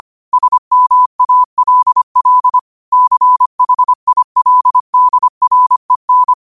Morse code, heard as a keyed tone, about 25 wpm, 1000 Hz. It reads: SMALL CHILDREN
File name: qotdfiles / 25wpm